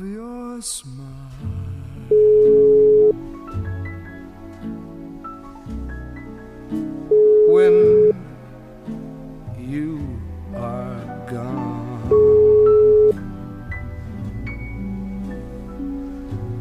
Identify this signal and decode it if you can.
Ringback tone (Europe, 425 Hz), with speech.